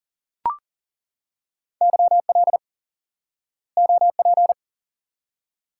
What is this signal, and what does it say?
Morse code, heard as a keyed tone, about 40 wpm, 700 Hz. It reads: YL OP
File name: IQE Course 3 - Lesson 89 - YAGI - Two Elements - 40wpm